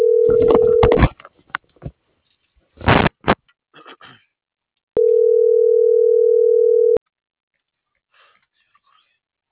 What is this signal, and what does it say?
Ringback tone (North America), 440 + 480 Hz.